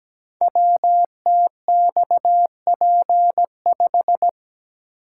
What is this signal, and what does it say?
Morse code, heard as a keyed tone, about 17 wpm, 690 Hz. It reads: WTXP5
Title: Code Groups - 5 Letter:Numbers - 17wpm